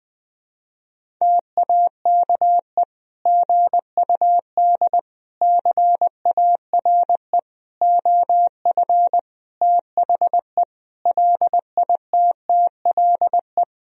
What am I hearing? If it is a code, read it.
Morse code, heard as a keyed tone, about 20 wpm, 700 Hz. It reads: TAKE GUD CARE OF THE LITTLE
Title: Sentences from Top 400 Words and Abbreviations 20wpm